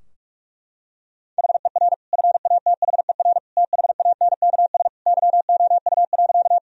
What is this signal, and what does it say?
Morse code, heard as a keyed tone, about 41 wpm, 700 Hz. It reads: HER FATHER THANKS YOU.
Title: Sentences from Top 1000 Words - Review Part 3 - 45wpm